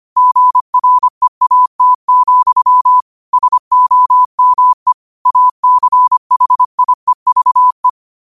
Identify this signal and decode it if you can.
Morse code, heard as a keyed tone, about 25 wpm, 1000 Hz. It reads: GREAT, SOME ACHIEVE
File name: qotdfiles / 25wpm